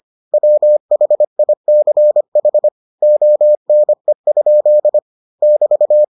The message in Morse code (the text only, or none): WHICH ONE? <BT>